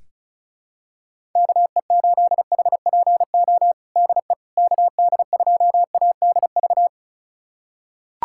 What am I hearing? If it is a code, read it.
Morse code, heard as a keyed tone, about 34 wpm, 700 Hz. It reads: KE8HPO DE KD2ADV
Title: USA Callsign DE Callsign - 6 Characters - 35wpm